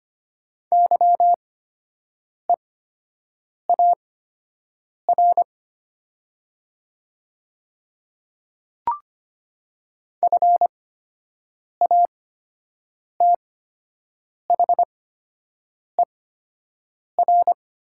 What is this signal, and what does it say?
Morse code, heard as a keyed tone, about 25 wpm, spaced as for 7 wpm, 690 Hz. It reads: YEAR FATHER
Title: 8x Character Spacing - Top 200 Words - 25wpm